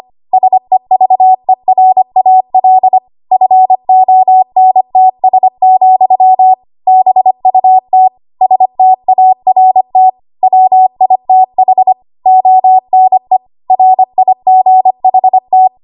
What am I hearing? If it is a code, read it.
Morse code, heard as a keyed tone, about 25 wpm, 750 Hz. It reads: SEVERAL FONTS, BUT START WITH ONE RIGHT
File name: Archive / 25 WPM / 25WPM